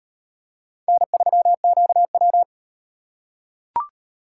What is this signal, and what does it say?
Morse code, heard as a keyed tone, about 36 wpm, 700 Hz. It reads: N3QW